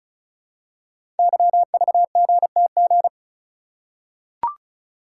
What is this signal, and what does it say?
Morse code, heard as a keyed tone, about 36 wpm, 700 Hz. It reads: YVGTG
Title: Code Groups - 5 Letters - 35wpm